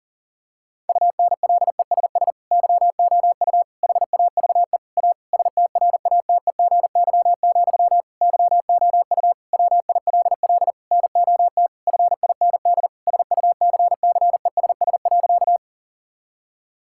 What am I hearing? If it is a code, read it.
Morse code, heard as a keyed tone, about 40 wpm, 700 Hz. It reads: UNLESS YOU HAVE A STRATEGY, YOU WILL NOT FIND SUCCESS.
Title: Sentences from Top 1000 Words - Review Part 5 - 40wpm